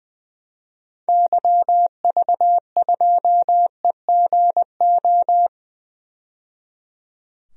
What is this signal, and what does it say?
Morse code, heard as a keyed tone, about 20 wpm, 700 Hz. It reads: YV2EGO
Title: DXCC-100 Call Signs - 6 Characters - 20wpm